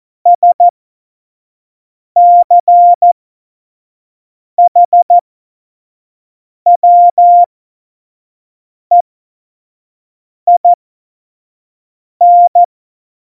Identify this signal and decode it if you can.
Morse code, heard as a keyed tone, about 14 wpm, spaced as for 5 wpm, 700 Hz. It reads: SCHWEIN
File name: Morsekurs / Witze / Witz 24 April 2018, 25 F70